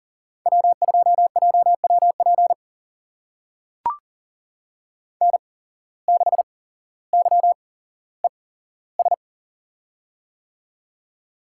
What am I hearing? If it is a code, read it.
Morse code, heard as a keyed tone, about 40 wpm, spaced as for 17 wpm, 700 Hz. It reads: W2JWP N6YES
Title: USA Call Signs - 5 Characters - 8x Character Spacing - 40wpm